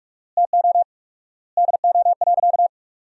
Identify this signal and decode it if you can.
Morse code, heard as a keyed tone, about 44 wpm, 700 Hz. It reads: TO DO.
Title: 4x Word-Spacing - Sentences from Top 100 Words - 45wpm